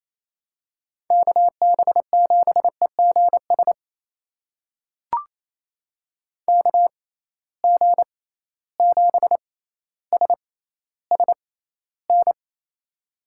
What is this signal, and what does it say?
Morse code, heard as a keyed tone, about 28 wpm, spaced as for 14 wpm, 700 Hz. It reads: KB7EGS KG7SSN
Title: USA Call Signs - 6x Character Spacing - 28wpm